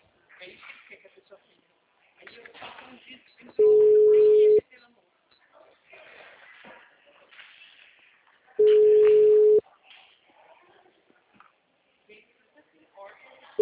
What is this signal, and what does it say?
Ringback tone (Europe, 425 Hz), with speech.